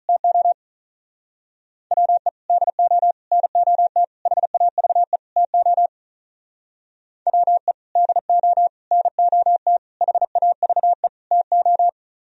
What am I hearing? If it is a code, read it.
Morse code, heard as a keyed tone, about 43 wpm, 700 Hz. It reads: TO WE DO NOT HAVE TO WE DO NOT HAVE TO
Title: Speed-Racing - Sets of 5 Words 35wpm